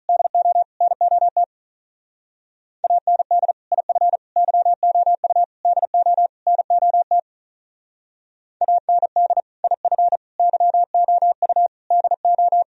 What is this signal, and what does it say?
Morse code, heard as a keyed tone, about 42 wpm, 700 Hz. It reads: DO NOT AND IF YOU DO NOT AND IF YOU DO
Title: Speed-Racing - Sets of 5 Words 35wpm